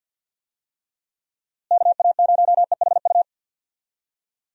Morse code, heard as a keyed tone, about 52 wpm, 700 Hz. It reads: KA0EHU
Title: USA Call Signs - 6 Characters - Repeated 3x - 50wpm